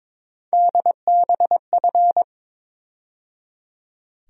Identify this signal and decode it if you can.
Morse code, heard as a keyed tone, about 21 wpm, 700 Hz. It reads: DBF
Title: USA Call Sign Postfixes - 3 Letters 22wpm